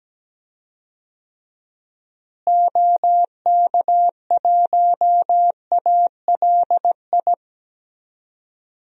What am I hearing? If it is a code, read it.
Morse code, heard as a keyed tone, about 17 wpm, 700 Hz. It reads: OK1ALI
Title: DXCC-100 Call Signs - 6 Characters - Repeated 3x - 17wpm